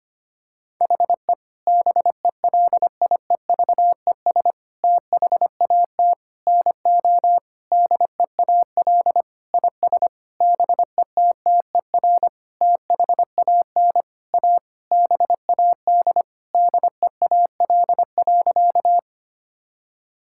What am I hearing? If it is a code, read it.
Morse code, heard as a keyed tone, about 25 wpm, 700 Hz. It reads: HE BELIEVES THAT NO DEAL IS BETTER THAN A BAD DEAL.
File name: practice / 25wpm / Sentences with 9-Word Phrases - 25wpm